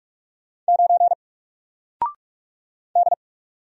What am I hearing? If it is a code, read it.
Morse code, heard as a keyed tone, about 45 wpm, 700 Hz. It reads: 9 D
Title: Single Letter-Number 45wpm